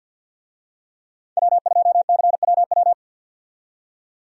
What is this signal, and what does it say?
Morse code, heard as a keyed tone, about 50 wpm, 700 Hz. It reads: W2CWW
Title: USA Call Signs - 5 Characters - 50wpm